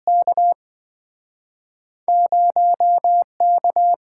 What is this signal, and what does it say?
Morse code, heard as a keyed tone, about 27 wpm, spaced as for 24 wpm, 700 Hz. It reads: K 0K
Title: Speed-Racing - Two Letter-Number Combinations - 20wpm